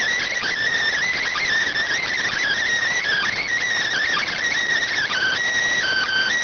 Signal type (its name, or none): Emergency Alert System (SAME) tones